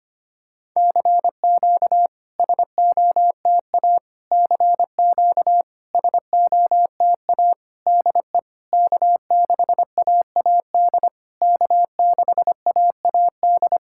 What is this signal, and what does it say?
Morse code, heard as a keyed tone, about 25 wpm, 690 Hz. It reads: CQ SOTA CQ SOTA DE K6AAD K6AAD
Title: SOTA QSO Practice 25wpm